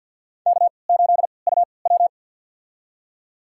Morse code, heard as a keyed tone, about 52 wpm, spaced as for 28 wpm, 700 Hz. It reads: K8UW
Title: USA Call Signs - 3x Character Spacing - 50wpm